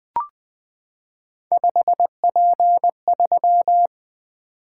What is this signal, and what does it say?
Morse code, heard as a keyed tone, about 20 wpm, 700 Hz. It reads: 5P3
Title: Three Letter-Number Combinations - 20wpm